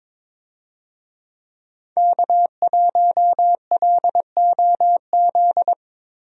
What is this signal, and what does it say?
Morse code, heard as a keyed tone, about 21 wpm, 700 Hz. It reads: K1LOZ
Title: USA Call Signs - 5 Characters - Repeated 3x - 22wpm